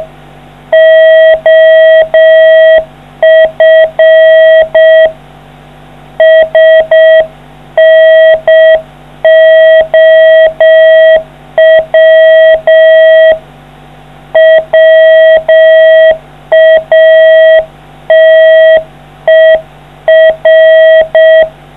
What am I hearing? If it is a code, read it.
Morse code, heard as a keyed tone, about 6 wpm, 660 Hz. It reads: OF SNOW WATER